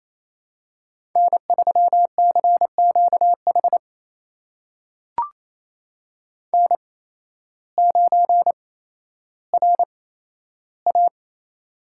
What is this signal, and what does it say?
Morse code, heard as a keyed tone, about 28 wpm, spaced as for 13 wpm, 700 Hz. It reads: N3CQH N9RA
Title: USA Call Signs - 8x Character Spacing - 28wpm